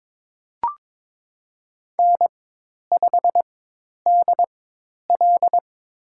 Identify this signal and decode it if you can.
Morse code, heard as a keyed tone, about 21 wpm, spaced as for 10 wpm, 700 Hz. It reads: N5DL
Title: USA Call Signs - 4x Character Spacing - 22wpm